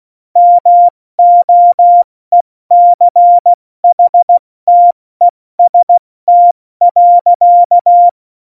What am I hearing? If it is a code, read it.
Morse code, heard as a keyed tone, about 16 wpm, spaced as for 14 wpm, 700 Hz. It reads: MOECHTEST.
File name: Morsekurs / Witze / Witz 26. Januar 2021, 70 F80